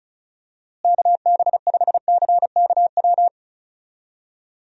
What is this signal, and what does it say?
Morse code, heard as a keyed tone, about 36 wpm, 700 Hz. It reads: KB5CKW
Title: USA Call Signs - 6 Characters - Repeated 2x - 35wpm